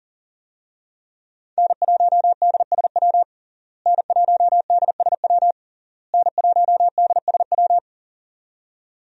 Morse code, heard as a keyed tone, about 40 wpm, 700 Hz. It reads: N1DSW N1DSW N1DSW
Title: USA Call Signs - 5 Characters - Repeated 3x - 40wpm